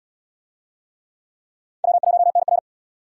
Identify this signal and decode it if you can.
Morse code, heard as a keyed tone, about 68 wpm, 700 Hz. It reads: W1NR